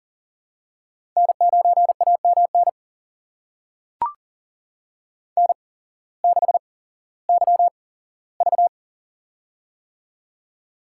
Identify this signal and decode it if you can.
Morse code, heard as a keyed tone, about 38 wpm, spaced as for 18 wpm, 700 Hz. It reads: N9AMN N6YV
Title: USA Call Signs - 8x Character Spacing - 40wpm